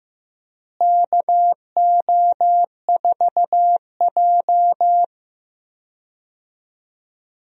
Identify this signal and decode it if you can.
Morse code, heard as a keyed tone, about 15 wpm, 700 Hz. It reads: KO4J
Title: USA Call Signs - 4 Characters - Repeated 2x - 15wpm